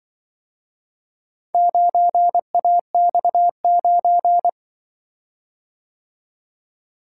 Morse code, heard as a keyed tone, about 23 wpm, 700 Hz. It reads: 9AX9
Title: Speed-Racing - Four Letter-Number Combinations - 20wpm